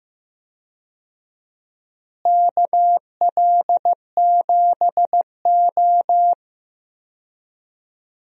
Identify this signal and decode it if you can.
Morse code, heard as a keyed tone, about 15 wpm, 700 Hz. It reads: KL7O
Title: DXCC-100 Call Signs - 4 Characters - Repeated 3x - 15wpm